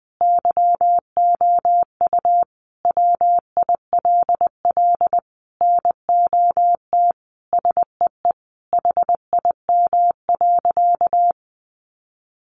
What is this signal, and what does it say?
Morse code, heard as a keyed tone, about 20 wpm, 700 Hz. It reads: YOU WILL NOT SEE HIM.
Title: Sentences from Top 600 Words 20wpm